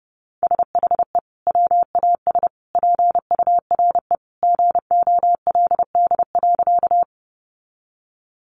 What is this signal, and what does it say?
Morse code, heard as a keyed tone, about 30 wpm, 700 Hz. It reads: SHE WAS PURE GOLD.
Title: Sentences from Top 1500 Words 30wpm